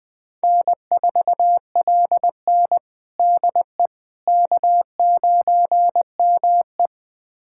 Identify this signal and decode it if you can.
Morse code, heard as a keyed tone, about 20 wpm, 700 Hz. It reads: N4LN DE K9ME
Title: USA Callsign DE Callsign - 4 Characters - 20wpm